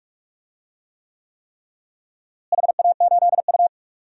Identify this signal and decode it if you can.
Morse code, heard as a keyed tone, about 46 wpm, 700 Hz. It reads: HA8U
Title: DXCC-100 Call Signs - 4 Characters - Repeated 3x - 45wpm